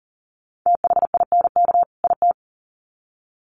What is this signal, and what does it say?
Morse code, heard as a keyed tone, about 40 wpm, 700 Hz. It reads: THINK IT
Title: Sets of 2 Words 40wpm